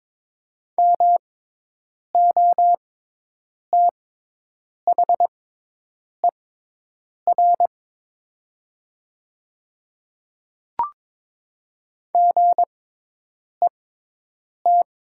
Morse code, heard as a keyed tone, about 22 wpm, spaced as for 8 wpm, 700 Hz. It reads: MOTHER GET
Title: 6x Character Spacing - Top 200 Words - 22wpm